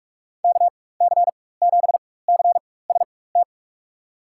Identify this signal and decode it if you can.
Morse code, heard as a keyed tone, about 41 wpm, spaced as for 20 wpm, 700 Hz. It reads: KC7CST
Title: USA Call Signs - 6 Characters - 4x Character Spacing - 45wpm